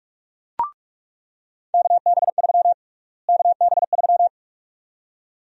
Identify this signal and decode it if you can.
Morse code, heard as a keyed tone, about 46 wpm, 700 Hz. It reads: KB3 KB3
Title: USA Call Sign Prefixes - Repeated 2x - 45wpm